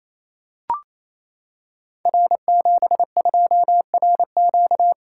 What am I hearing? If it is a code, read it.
Morse code, heard as a keyed tone, about 27 wpm, 700 Hz. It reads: R72RQ
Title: Code Groups - 5 Letter:Numbers - 28wpm